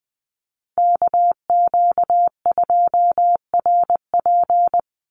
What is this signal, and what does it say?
Morse code, heard as a keyed tone, about 20 wpm, 700 Hz. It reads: KQ2RP
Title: NA SOTA Chaser Callsigns 20wpm